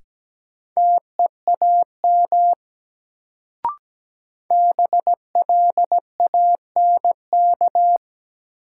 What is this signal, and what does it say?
Morse code, heard as a keyed tone, about 17 wpm, 700 Hz. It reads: TEAM BLANK
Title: Sentences from Top 1500 Words - New Word Drill Alt - 17wpm